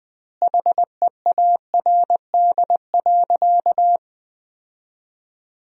Morse code, heard as a keyed tone, about 20 wpm, 700 Hz. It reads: HEARD.
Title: 2x Word-Spacing - Sentences from Top 400 Words - 20wpm